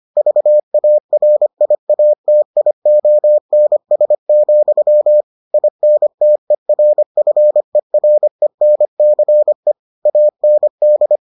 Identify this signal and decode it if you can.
Morse code, heard as a keyed tone, about 25 wpm, 600 Hz. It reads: VARIATIONS, INTERFERENCE AND